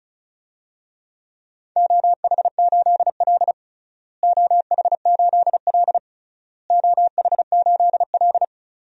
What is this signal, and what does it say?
Morse code, heard as a keyed tone, about 34 wpm, 700 Hz. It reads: OH8L OH8L OH8L
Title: DXCC-100 Call Signs - 4 Characters - Repeated 3x - 35wpm